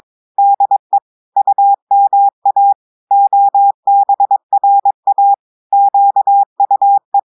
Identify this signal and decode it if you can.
Morse code, heard as a keyed tone, about 22 wpm, 800 Hz. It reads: DE UMA OBRA QUE
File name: audios / portugais / portugais texte en clair 1 vitesse 22 mots minutes